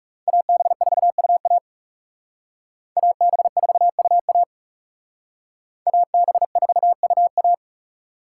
Morse code, heard as a keyed tone, about 38 wpm, 700 Hz. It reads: AB4UA AB4UA AB4UA